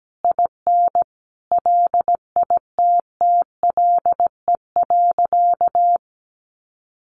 Morse code, heard as a keyed tone, about 17 wpm, 700 Hz. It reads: IN LITTLE.
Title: Sentences from Top 1500 Words 17wpm